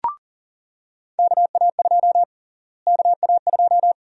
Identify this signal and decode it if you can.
Morse code, heard as a keyed tone, about 40 wpm, 700 Hz. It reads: KA2 KA2
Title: USA Call Sign Prefixes - Repeated 2x - 40wpm